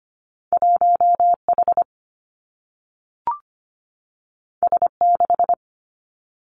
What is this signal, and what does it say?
Morse code, heard as a keyed tone, about 25 wpm, 700 Hz. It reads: 1 H S 6
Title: Two Challenging Letter-Number Combinations 25wpm